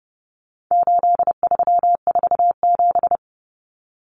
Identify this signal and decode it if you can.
Morse code, heard as a keyed tone, about 30 wpm, 700 Hz. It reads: 8347
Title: Numbers - 4-Digit 30wpm